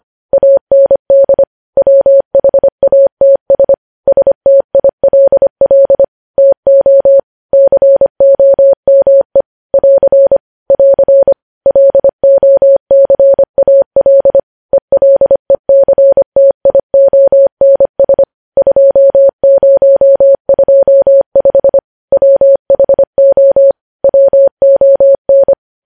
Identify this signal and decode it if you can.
Morse code, heard as a keyed tone, about 25 wpm, 550 Hz. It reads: AND WHATS STILL TO COME <AR> <AR> LOCAL ELECTIONS 2025 WHO WON
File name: podcast / News headlines-1x-25x25